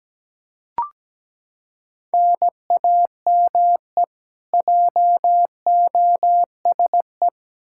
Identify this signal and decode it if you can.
Morse code, heard as a keyed tone, about 17 wpm, 700 Hz. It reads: NAME JOSE
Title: Practice QSO - Exchange 1 - Beginning - 17wpm